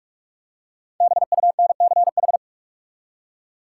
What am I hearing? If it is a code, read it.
Morse code, heard as a keyed tone, about 46 wpm, 700 Hz. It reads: BUNCH
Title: Five Letter Words - 45wpm